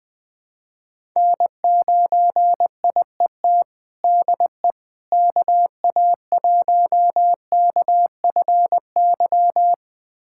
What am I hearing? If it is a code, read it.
Morse code, heard as a keyed tone, about 20 wpm, 700 Hz. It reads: N9IET DE KA1KFY
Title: USA Callsign DE Callsign 20wpm